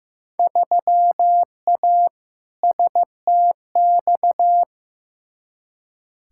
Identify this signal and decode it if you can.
Morse code, heard as a keyed tone, about 16 wpm, 700 Hz. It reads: 3A STX
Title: Field Day Exchanges - 15wpm